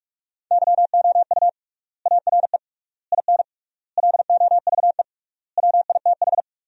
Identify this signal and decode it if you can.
Morse code, heard as a keyed tone, about 41 wpm, 700 Hz. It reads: YOU ARE IN LOVE WITH
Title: 3x Word-Spacing - Sentences from Top 400 Words - 45wpm